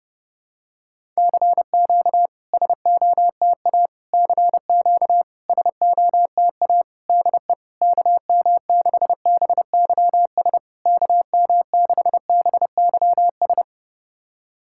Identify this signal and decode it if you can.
Morse code, heard as a keyed tone, about 30 wpm, 690 Hz. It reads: CQ SOTA CQ SOTA DE KM6BYS KM6BYS
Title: SOTA QSO Practice 30wpm